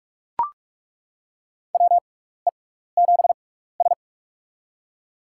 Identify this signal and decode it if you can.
Morse code, heard as a keyed tone, about 42 wpm, spaced as for 15 wpm, 690 Hz. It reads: WE7S
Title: USA Call Signs - 6x Character Spacing - 45wpm